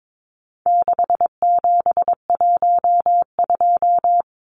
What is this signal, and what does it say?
Morse code, heard as a keyed tone, about 22 wpm, 700 Hz. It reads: 6712
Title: Numbers - 4-Digit 22wpm